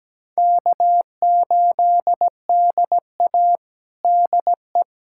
Morse code, heard as a keyed tone, about 17 wpm, 700 Hz. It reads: K8DA DE
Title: USA Callsign DE Callsign - 4 Characters - 17wpm